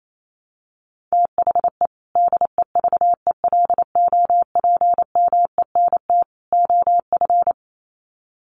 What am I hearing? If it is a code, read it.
Morse code, heard as a keyed tone, about 28 wpm, 700 Hz. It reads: THE DEVELOPMENT OF
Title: Sets of 3 Words 28wpm